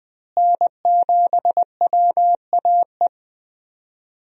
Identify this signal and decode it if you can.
Morse code, heard as a keyed tone, about 21 wpm, 700 Hz. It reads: N7WAE
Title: USA Call Signs - 5 Characters - Repeated 2x - 20wpm